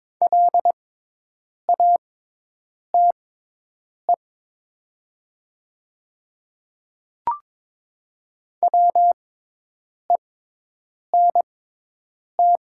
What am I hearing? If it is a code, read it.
Morse code, heard as a keyed tone, about 22 wpm, spaced as for 8 wpm, 700 Hz. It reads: LATE WENT
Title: 6x Character Spacing - Top 500 Words - 22wpm